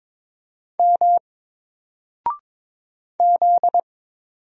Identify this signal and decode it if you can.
Morse code, heard as a keyed tone, about 22 wpm, 700 Hz. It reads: M Z
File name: Single-Letter-Number-22wpm